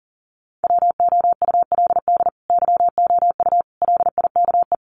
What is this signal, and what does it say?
Morse code, heard as a keyed tone, about 40 wpm, 700 Hz. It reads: WOULD YOU LIKE
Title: Sentences from Top 1000 Words 40wpm